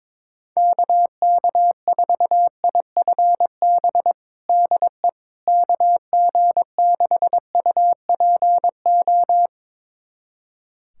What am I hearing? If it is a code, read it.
Morse code, heard as a keyed tone, about 22 wpm, 700 Hz. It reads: KK4IFB DE KG6UPO
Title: USA Callsign DE Callsign - 6 Characters - 22wpm